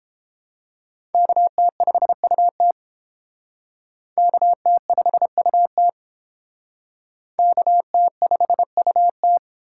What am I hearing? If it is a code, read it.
Morse code, heard as a keyed tone, about 28 wpm, 700 Hz. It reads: KT5UT KT5UT KT5UT